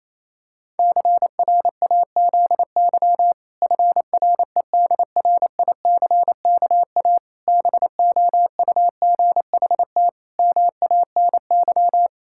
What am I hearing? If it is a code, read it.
Morse code, heard as a keyed tone, about 28 wpm, 700 Hz. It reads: CRAZY FREDRICKA BOUGHT MANY
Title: Pangram Challenge v2 - 28wpm